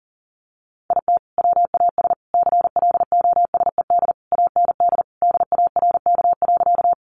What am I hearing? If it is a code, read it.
Morse code, heard as a keyed tone, about 40 wpm, 700 Hz. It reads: IT WAS CLOSED AND DARK.
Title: Sentences from Top 1400 Words 40wpm